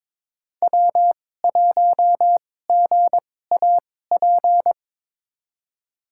Morse code, heard as a keyed tone, about 23 wpm, spaced as for 16 wpm, 690 Hz. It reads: W1GAP
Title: USA Call Signs - 5 Characters - 2x Character Spacing - 22wpm